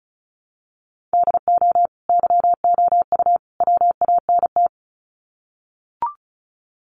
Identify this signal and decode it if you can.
Morse code, heard as a keyed tone, about 34 wpm, 700 Hz. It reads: DO YOU WANT
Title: Sets of 3 Words 35wpm